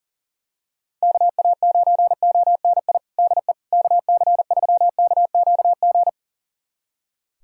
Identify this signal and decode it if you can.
Morse code, heard as a keyed tone, about 40 wpm, 700 Hz. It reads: KA9ONI DE KC3KQG
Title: USA Callsign DE Callsign - 6 Characters - 40wpm